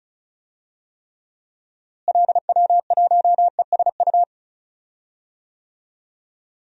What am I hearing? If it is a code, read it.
Morse code, heard as a keyed tone, about 34 wpm, 700 Hz. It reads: LW1ESU
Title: DXCC-100 Call Signs - 6 Characters - Repeated 3x - 35wpm